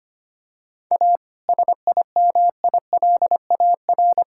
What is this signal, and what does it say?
Morse code, heard as a keyed tone, about 25 wpm, 700 Hz. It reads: A SIMILAR
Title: Sentences from Top 1000 Words - Review Part 5 - 25wpm